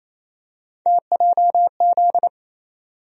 Morse code, heard as a keyed tone, about 28 wpm, 700 Hz. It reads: TJZ
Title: USA Call Sign Postfixes - 3 Letters 28wpm